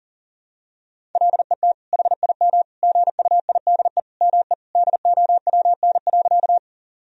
Morse code, heard as a keyed tone, about 40 wpm, 700 Hz. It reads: LET HIM GUIDE ME DOWN.
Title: Sentences from Top 1000 Words - Review Part 3 - 40wpm